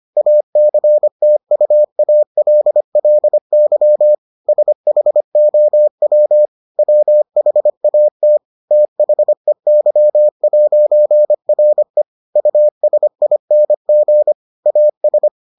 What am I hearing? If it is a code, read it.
Morse code, heard as a keyed tone, about 25 wpm, 600 Hz. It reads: ACTUALLY SHOW WHAT THEY'RE USING AS